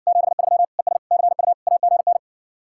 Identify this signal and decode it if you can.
Morse code, heard as a keyed tone, about 60 wpm, 700 Hz. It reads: 73 ES CU AGN